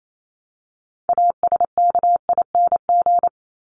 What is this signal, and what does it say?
Morse code, heard as a keyed tone, about 28 wpm, 700 Hz. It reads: ASKING
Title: ING Suffix - Top 100 Words with Base Words 28wpm